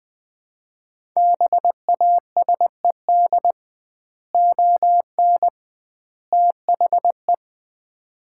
Morse code, heard as a keyed tone, about 20 wpm, 700 Hz. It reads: BASED ON THE
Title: 2x Word-Spacing - Sets of 3 Words - 20wpm